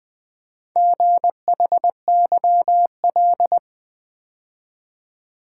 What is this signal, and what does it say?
Morse code, heard as a keyed tone, about 20 wpm, 700 Hz. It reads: GHYL
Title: Four Letter-Number Combinations - 20wpm